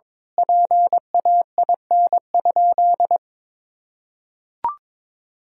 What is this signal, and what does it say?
Morse code, heard as a keyed tone, about 22 wpm, 700 Hz. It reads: PAIN?
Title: Sentences from Top 1600 Words 22wpm